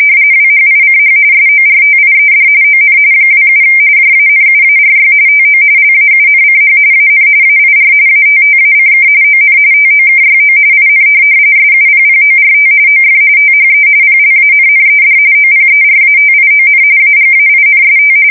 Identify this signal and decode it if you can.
RTTY, about 45.45 baud. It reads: WHICH SORTIED ON 27 JANUARY TO LAUNCH AN AIR ATTACK ON TOKYO IN SUPPORT OF THE PLANNED OPERATION AGAINST
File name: fsk170 / USS-Missouri